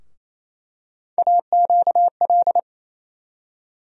Morse code, heard as a keyed tone, about 27 wpm, 700 Hz. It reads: AQL
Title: USA Call Sign Postfixes - 3 Letters 28wpm